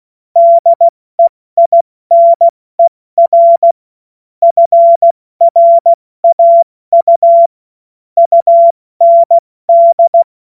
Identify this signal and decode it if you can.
Morse code, heard as a keyed tone, about 16 wpm, 680 Hz. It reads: DEINER FRAU UND
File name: Morsekurs / Witze / Witz 09. August 2022 70 F80